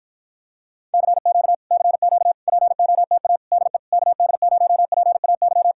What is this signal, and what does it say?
Morse code, heard as a keyed tone, about 52 wpm, 700 Hz. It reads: CQ CQ POTA DE KD0PAY